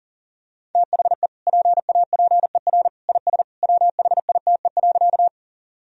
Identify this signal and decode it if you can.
Morse code, heard as a keyed tone, about 40 wpm, 700 Hz. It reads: THE PAPER IS WHITE.
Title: Sentences from Top 1000 Words - Review Part 1 - 40wpm